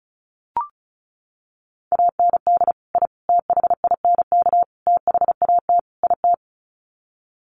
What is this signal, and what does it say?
Morse code, heard as a keyed tone, about 34 wpm, 700 Hz. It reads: AND I THINK THAT IT
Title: Sets of 5 Words Encore2 35wpm